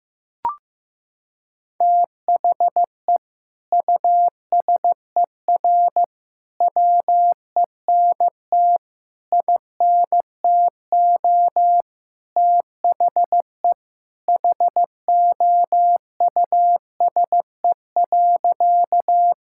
Morse code, heard as a keyed tone, about 15 wpm, 700 Hz. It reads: THE USER WENT INTO THE HOUSE.
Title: Sentences from Top 1000 Words - Review Part 5 - 15wpm